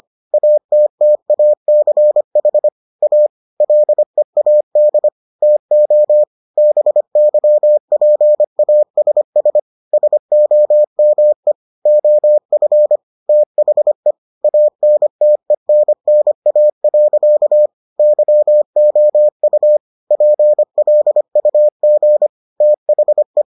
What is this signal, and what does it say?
Morse code, heard as a keyed tone, about 25 wpm, 600 Hz. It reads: ATTACH A LEAD TO BYPASS SOME OF THE ANTENNA. YOU PLUG THE